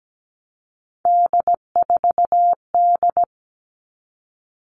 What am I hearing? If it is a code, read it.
Morse code, heard as a keyed tone, about 17 wpm, 700 Hz. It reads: D4D
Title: All Cut Numbers - 3-Digit 17wpm